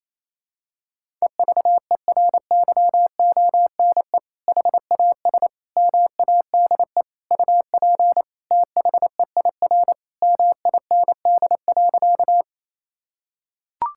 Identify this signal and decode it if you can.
Morse code, heard as a keyed tone, about 28 wpm, 700 Hz. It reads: EVERYONE HAS MADE UP THEIR MIND.
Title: Sentences from Top 1000 Words - Review Part 5 - 28wpm